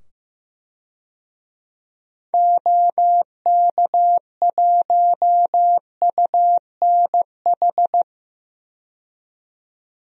Morse code, heard as a keyed tone, about 16 wpm, 700 Hz. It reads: OK1UNH
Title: DXCC-100 Call Signs - 6 Characters - 15wpm